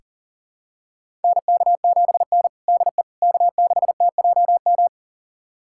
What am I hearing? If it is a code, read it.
Morse code, heard as a keyed tone, about 40 wpm, 700 Hz. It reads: NK7N DE K6TJM
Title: USA Callsign DE Callsign 40wpm